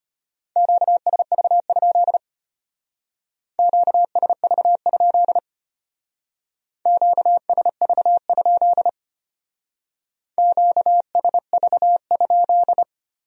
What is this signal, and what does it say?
Morse code, heard as a keyed tone, about 32 wpm, 700 Hz. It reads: QSV? QSV? QSV? QSV?